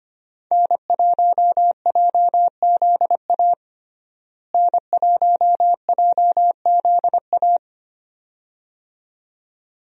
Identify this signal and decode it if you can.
Morse code, heard as a keyed tone, about 25 wpm, 700 Hz. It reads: N1JZA N1JZA
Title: USA Call Signs - 5 Characters - Repeated 2x - 25wpm